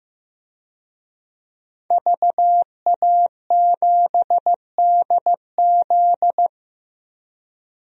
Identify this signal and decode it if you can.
Morse code, heard as a keyed tone, about 15 wpm, 700 Hz. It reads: VA7DZ
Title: DXCC-100 Call Signs - 5 Characters - 15wpm